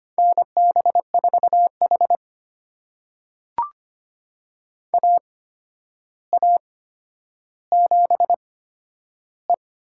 Morse code, heard as a keyed tone, about 25 wpm, spaced as for 11 wpm, 700 Hz. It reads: NB4H AA7E
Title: USA Call Signs - 4 Characters - 8x Character Spacing - 25wpm